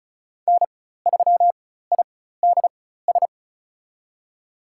Morse code, heard as a keyed tone, about 34 wpm, spaced as for 16 wpm, 700 Hz. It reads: N3IDS
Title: USA Call Signs - 5 Characters - 4x Character Spacing - 35wpm